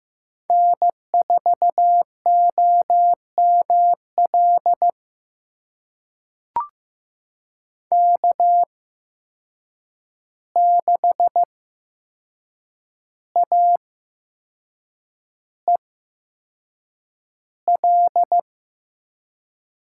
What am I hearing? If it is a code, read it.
Morse code, heard as a keyed tone, about 15 wpm, spaced as for 6 wpm, 700 Hz. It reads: N4OML K6AEL
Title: USA Call Signs - 5 Characters - 8x Character Spacing - 15wpm